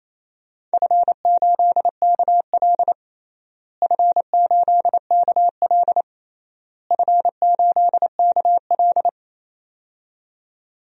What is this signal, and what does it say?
Morse code, heard as a keyed tone, about 28 wpm, 700 Hz. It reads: F8KL F8KL F8KL
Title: DXCC-100 Call Signs - 4 Characters - Repeated 3x - 28wpm